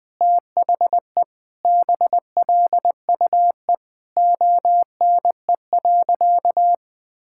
Morse code, heard as a keyed tone, about 20 wpm, 700 Hz. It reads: THE BLUE ONE.
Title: Sentences with 2-Word Phrases - 20wpm